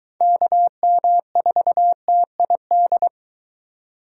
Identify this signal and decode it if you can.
Morse code, heard as a keyed tone, about 22 wpm, 700 Hz. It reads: KM4TID